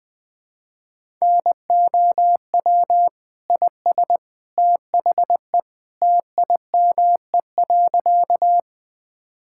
Morse code, heard as a keyed tone, about 20 wpm, 700 Hz. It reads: NOW IS THE TIME.
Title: Sentences with 2-Word Phrases - 20wpm